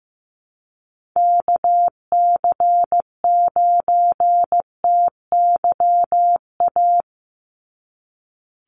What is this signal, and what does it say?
Morse code, heard as a keyed tone, about 15 wpm, 700 Hz. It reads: KC9TYA
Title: NA SOTA Chaser Callsigns 15wpm